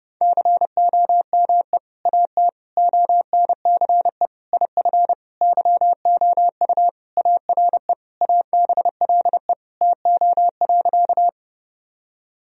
Morse code, heard as a keyed tone, about 32 wpm, 700 Hz. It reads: COME AT ONCE IF YOU ARE ABLE TO.
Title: Sentences from Top 1000 Words - Review Part 2 - 30wpm